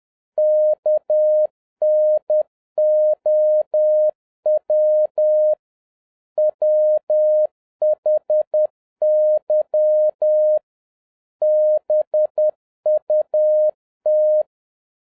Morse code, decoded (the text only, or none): KNOW WHY BUT